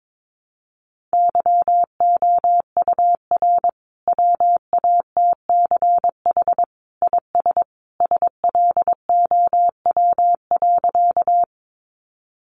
Morse code, heard as a keyed tone, about 22 wpm, 700 Hz. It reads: YOUR WATCH IS SLOW.
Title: Sentences from Top 400 Words 22wpm